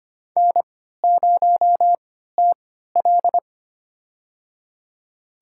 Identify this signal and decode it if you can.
Morse code, heard as a keyed tone, about 25 wpm, spaced as for 14 wpm, 700 Hz. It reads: N0TL
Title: USA Call Signs - 4 Characters - 3x Character Spacing - 25wpm